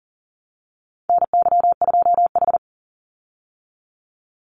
Morse code, heard as a keyed tone, about 40 wpm, 700 Hz. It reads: NY2H
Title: Call Signs Repeated 2x - 40wpm